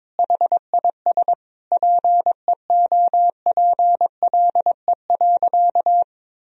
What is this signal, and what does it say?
Morse code, heard as a keyed tone, about 22 wpm, 700 Hz. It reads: HIS PEOPLE.
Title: Sentences from Top 1000 Words - Review Part 3 - 22wpm